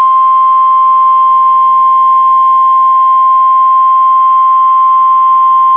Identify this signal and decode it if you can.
Emergency Alert System (SAME) tones (the 1050 Hz attention tone). It is heard from 0.6 to 4.9 s.